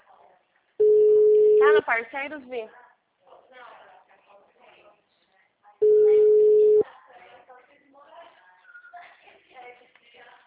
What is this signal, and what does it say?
Ringback tone (Europe, 425 Hz), with speech.